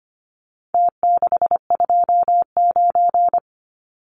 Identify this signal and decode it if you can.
Morse code, heard as a keyed tone, about 25 wpm, 700 Hz. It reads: T629
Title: Numbers - Contest Style - 4-Digit 25wpm